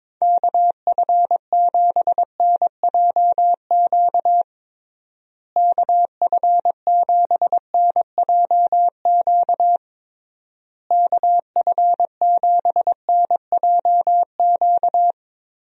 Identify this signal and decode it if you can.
Morse code, heard as a keyed tone, about 22 wpm, 700 Hz. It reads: KF7NJQ KF7NJQ KF7NJQ
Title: USA Call Signs - 6 Characters - Repeated 3x - 22wpm